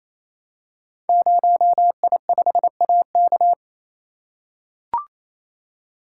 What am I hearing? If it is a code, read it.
Morse code, heard as a keyed tone, about 27 wpm, 700 Hz. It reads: 0I5AK
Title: Code Groups - 5 Letter:Numbers - 28wpm